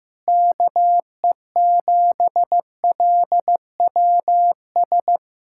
Morse code, heard as a keyed tone, about 15 wpm, 700 Hz. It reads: KE7LWS
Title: Speed-Racing Common QSO Phrases Encore - 15wpm